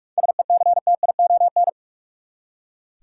Morse code, heard as a keyed tone, about 44 wpm, 700 Hz. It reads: SECTION
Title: Seven Letter Words - 45wpm